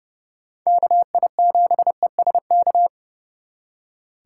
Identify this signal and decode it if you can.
Morse code, heard as a keyed tone, about 38 wpm, 700 Hz. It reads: KI7ESK